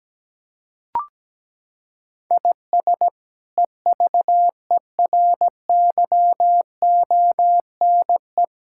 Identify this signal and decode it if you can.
Morse code, heard as a keyed tone, about 17 wpm, 700 Hz. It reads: IS EVERYONE
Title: Sentences from Top 1000 Words - Review Part 5 - 17wpm